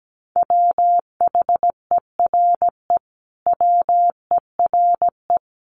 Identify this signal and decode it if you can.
Morse code, heard as a keyed tone, about 17 wpm, 700 Hz. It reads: WHERE WERE
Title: Sentences from Top 1000 Words 17wpm